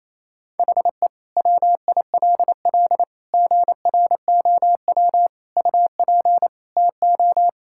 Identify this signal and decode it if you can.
Morse code, heard as a keyed tone, about 27 wpm, 700 Hz. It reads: HE WILL GROW UP TO
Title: Sentences from Top 1000 Words - Review Part 6 - 28wpm